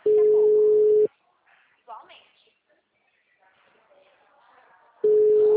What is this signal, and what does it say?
Ringback tone (Europe, 425 Hz), with speech.
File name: out- -2008- - - .56